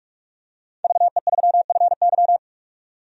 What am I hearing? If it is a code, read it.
Morse code, heard as a keyed tone, about 46 wpm, 700 Hz. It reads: VE3FY
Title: DXCC-100 Call Signs - 5 Characters - 45wpm